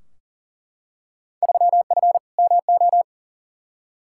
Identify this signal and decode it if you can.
Morse code, heard as a keyed tone, about 40 wpm, 700 Hz. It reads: 3F MO
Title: Field Day Exchanges Encore - 40wpm